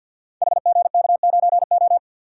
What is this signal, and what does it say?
Morse code, heard as a keyed tone, about 52 wpm, 700 Hz. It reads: HGK9O
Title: Code Groups - 5 Letter:Numbers - 50wpm